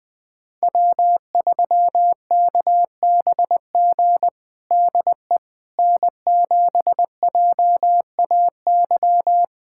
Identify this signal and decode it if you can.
Morse code, heard as a keyed tone, about 21 wpm, 700 Hz. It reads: W3KBG DE N7JAY
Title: USA Callsign DE Callsign - 5 Characters - 20wpm